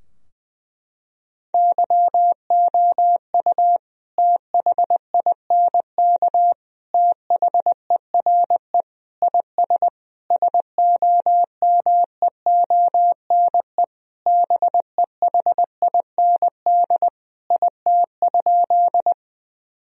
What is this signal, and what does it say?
Morse code, heard as a keyed tone, about 20 wpm, 700 Hz. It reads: YOU THINK THERE IS SOMEONE BEHIND IT?
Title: Sentences from Top 1000 Words - Review Part 4 - 20wpm